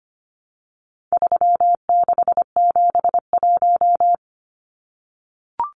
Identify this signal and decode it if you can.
Morse code, heard as a keyed tone, about 25 wpm, 700 Hz. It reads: 3671
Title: Numbers - 4-Digit 25wpm